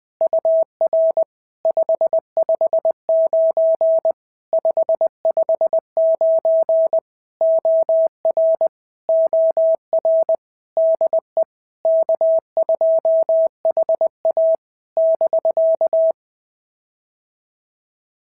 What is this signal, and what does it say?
Morse code, heard as a keyed tone, about 20 wpm, 650 Hz. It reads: UR 559 559 OR OR DE K2HA <BK>
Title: POTA QSO Practice - 20wpm